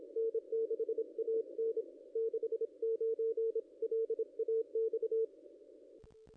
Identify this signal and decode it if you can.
Morse code, heard as a keyed tone, about 26 wpm, 450 Hz. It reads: N6AN 69 LAX
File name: contests / ss-cw-k5tr-op-n5rz -1700